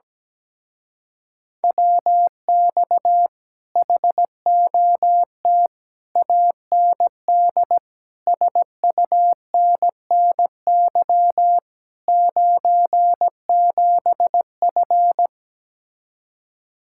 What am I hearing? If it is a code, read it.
Morse code, heard as a keyed tone, about 17 wpm, 700 Hz. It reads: WX HOT AND SUNNY 97F
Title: Speed-Racing Common QSO Phrases Encore - 17wpm